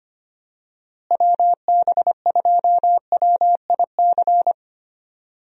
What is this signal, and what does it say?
Morse code, heard as a keyed tone, about 25 wpm, 700 Hz. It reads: WB2WIC
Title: USA Call Signs - 6 Characters - Repeated 3x - 25wpm